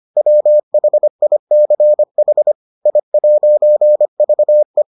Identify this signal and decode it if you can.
Morse code, heard as a keyed tone, about 25 wpm, 600 Hz. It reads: WHICH I'VE